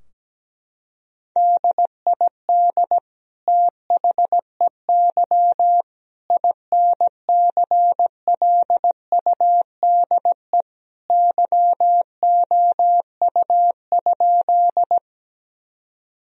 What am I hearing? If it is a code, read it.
Morse code, heard as a keyed tone, about 17 wpm, 700 Hz. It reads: DID THEY INCLUDE YOU?
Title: Sentences from Top 1000 Words - Review Part 3 - 17wpm